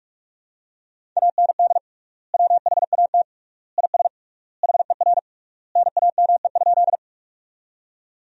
Morse code, heard as a keyed tone, about 41 wpm, 700 Hz. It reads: AND WHAT IS HER NAME?
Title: 3x Word-Spacing - Sentences from Top 200 Words - 45wpm